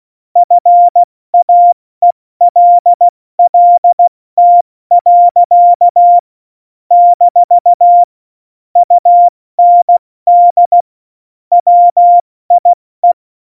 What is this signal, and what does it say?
Morse code, heard as a keyed tone, about 16 wpm, 700 Hz. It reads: FAELLT. - UND WIE
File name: Morsekurs / Witze / Witz 05. September 17, Tempo 70